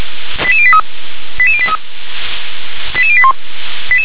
Five-tone selective call (ZVEI), heard as quick runs of five tones, tones of about 80 ms. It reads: 88083 88083 88081